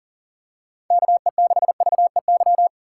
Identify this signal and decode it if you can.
Morse code, heard as a keyed tone, about 40 wpm, 700 Hz. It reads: KE6VEY
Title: USA Call Signs - 2x Character Spacing - 40wpm